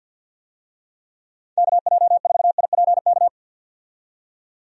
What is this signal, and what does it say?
Morse code, heard as a keyed tone, about 53 wpm, 700 Hz. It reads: KJ4IPK
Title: USA Call Signs - 6 Characters - 50wpm